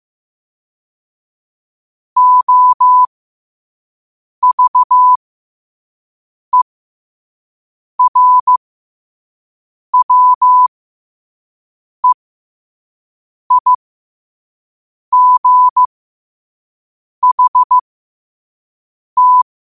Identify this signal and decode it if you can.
Morse code, heard as a keyed tone, about 15 wpm, spaced as for 5 wpm, 1000 Hz. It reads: OVERWEIGHT